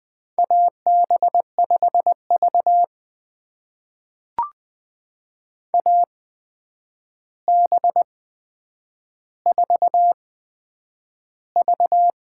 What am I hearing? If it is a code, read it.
Morse code, heard as a keyed tone, about 20 wpm, spaced as for 9 wpm, 700 Hz. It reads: AB5V AB4V
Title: USA Call Signs - 4 Characters - 8x Character Spacing - 20wpm